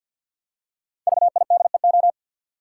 Morse code, heard as a keyed tone, about 53 wpm, 700 Hz. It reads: VIDEO
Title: Five Letter Words - 50wpm